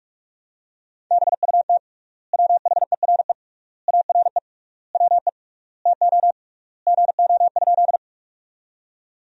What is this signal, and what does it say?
Morse code, heard as a keyed tone, about 41 wpm, 700 Hz. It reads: BUT WHERE ARE WE TO GO?
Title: 3x Word-Spacing - Sentences from Top 200 Words - 45wpm